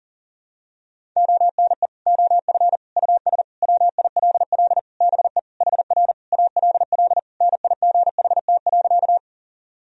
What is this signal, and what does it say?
Morse code, heard as a keyed tone, about 40 wpm, 700 Hz. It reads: ONE OF US WILL BE HR ALL NIGHT.
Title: Sentences from Top 300 Words and Abbreviations 40wpm